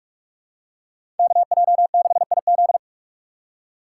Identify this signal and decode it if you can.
Morse code, heard as a keyed tone, about 41 wpm, 700 Hz. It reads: KJ6IZ
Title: USA Call Signs - 5 Characters - 3x Character Spacing - 45wpm